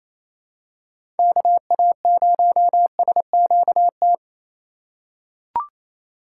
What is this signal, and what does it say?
Morse code, heard as a keyed tone, about 28 wpm, 700 Hz. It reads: KA0SQT
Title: USA Call Signs - 6 Characters - 3x Character Spacing - 28wpm